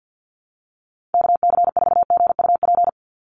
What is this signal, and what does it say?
Morse code, heard as a keyed tone, about 48 wpm, 700 Hz. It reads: KC4GUP
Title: USA Call Signs 50wpm